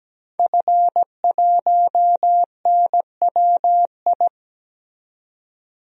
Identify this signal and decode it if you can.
Morse code, heard as a keyed tone, about 17 wpm, 700 Hz. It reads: F1NWI
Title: DXCC-100 Call Signs - 5 Characters - 17wpm